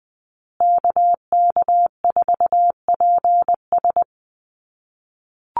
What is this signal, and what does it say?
Morse code, heard as a keyed tone, about 20 wpm, 700 Hz. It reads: KK4PS
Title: USA Call Signs Repeated 3x 20wpm